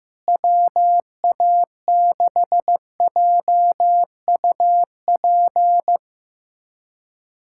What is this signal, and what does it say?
Morse code, heard as a keyed tone, about 15 wpm, 700 Hz. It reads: WA6JUP
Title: USA Callsign DE Callsign - 6 Characters - 15wpm